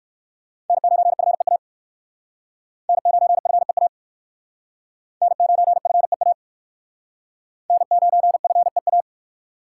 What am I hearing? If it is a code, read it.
Morse code, heard as a keyed tone, about 56 wpm, 700 Hz. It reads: N9FEA N9FEA N9FEA N9FEA